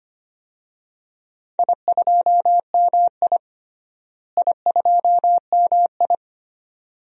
Morse code, heard as a keyed tone, about 25 wpm, 700 Hz. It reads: I2MI I2MI
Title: DXCC-100 Call Signs - 4 Characters - Repeated 2x - 25wpm